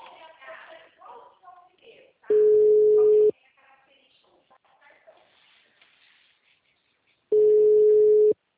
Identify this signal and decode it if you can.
Ringback tone (Europe, 425 Hz), with speech.